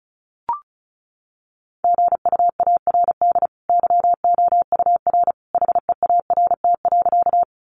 Morse code, heard as a keyed tone, about 34 wpm, 700 Hz. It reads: GUARD YOUR HEART.
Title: Sentences from Top 1200 Words 35wpm